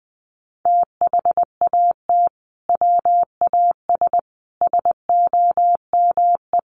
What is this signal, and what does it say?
Morse code, heard as a keyed tone, about 20 wpm, 700 Hz. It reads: THAT WAS SOME
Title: Sentences from Top 1400 Words 20wpm